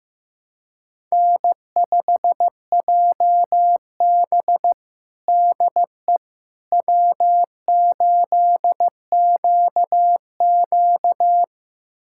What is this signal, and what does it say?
Morse code, heard as a keyed tone, about 15 wpm, 700 Hz. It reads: N5JB DE W8QQ
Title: USA Callsign DE Callsign - 4 Characters - 15wpm